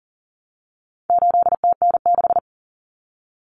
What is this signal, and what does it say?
Morse code, heard as a keyed tone, about 38 wpm, 700 Hz. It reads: 8TN6
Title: Common Cut Numbers - 4-Digit 40wpm